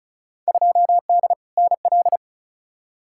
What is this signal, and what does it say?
Morse code, heard as a keyed tone, about 36 wpm, 700 Hz. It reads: 2D NL
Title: Field Day Exchanges Encore - 35wpm